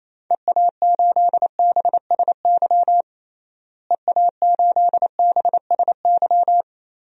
Morse code, heard as a keyed tone, about 28 wpm, 700 Hz. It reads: EA8BSY EA8BSY
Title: DXCC-100 Call Signs - 6 Characters - Repeated 2x - 28wpm